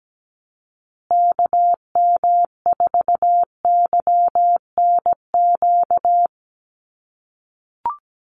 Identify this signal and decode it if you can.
Morse code, heard as a keyed tone, about 17 wpm, 700 Hz. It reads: KM4YNQ
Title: USA Call Signs Repeated 2x 17wpm